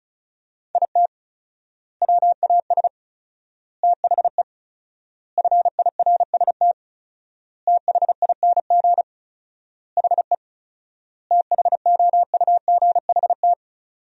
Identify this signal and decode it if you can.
Morse code, heard as a keyed tone, about 36 wpm, 700 Hz. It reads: IT WAS THE FIRST THING HE THOUGHT
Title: 4x Word-Spacing - Sentences from Top 300 Words - 35wpm